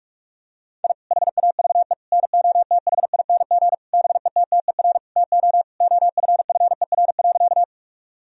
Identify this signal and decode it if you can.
Morse code, heard as a keyed tone, about 41 wpm, 700 Hz. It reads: I HAVE NOTHING BETTER TO OFFER.
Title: Sentences from Top 1000 Words - Review Part 4 - 45wpm